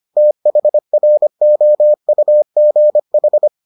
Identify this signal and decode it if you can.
Morse code, heard as a keyed tone, about 25 wpm, 600 Hz. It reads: THROUGH